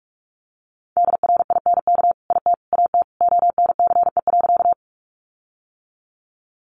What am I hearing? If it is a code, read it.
Morse code, heard as a keyed tone, about 44 wpm, 700 Hz. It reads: DRINK IT AT ONCE.
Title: Sentences from Top 1500 Words 45wpm